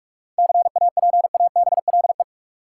Morse code, heard as a keyed tone, about 46 wpm, 700 Hz. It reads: CAPABLE
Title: Seven Letter Words - 45wpm